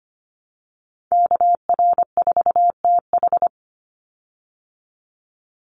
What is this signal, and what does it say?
Morse code, heard as a keyed tone, about 25 wpm, 700 Hz. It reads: KR4TH
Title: Call Signs Repeated 3x - 25wpm